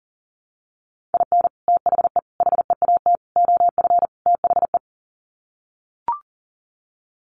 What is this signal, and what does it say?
Morse code, heard as a keyed tone, about 40 wpm, 700 Hz. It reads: IN THE HEAT OF THE
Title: Sets of 5 Words Encore2 40wpm